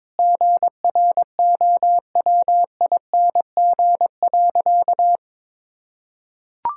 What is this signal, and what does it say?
Morse code, heard as a keyed tone, about 22 wpm, 700 Hz. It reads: GROWING.
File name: practice / 22wpm / Sentences with 2-Word Phrases - 22wpm